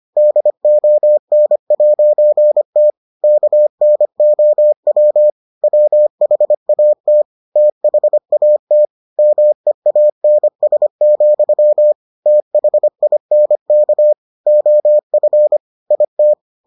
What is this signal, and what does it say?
Morse code, heard as a keyed tone, about 25 wpm, 600 Hz. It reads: DON'T KNOW WHAT THAT MEANS, THINK OF IT